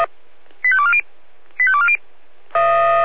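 Five-tone selective call (ZVEI), heard as quick runs of five tones, tones of about 70 ms. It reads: 85240 85240
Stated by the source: ZVEI five-tone call 88133 received by SDR.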